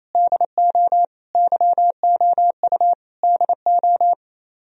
Morse code, heard as a keyed tone, about 36 wpm, 700 Hz. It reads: DO YOU DO